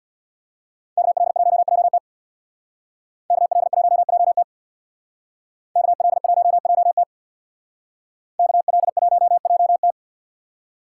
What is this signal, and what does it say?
Morse code, heard as a keyed tone, about 64 wpm, 700 Hz. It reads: KL1JT KL1JT KL1JT KL1JT